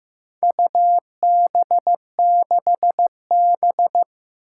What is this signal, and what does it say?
Morse code, heard as a keyed tone, about 15 wpm, 700 Hz. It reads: UB6B
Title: Four Challenging Letter-Number Combinations 15wpm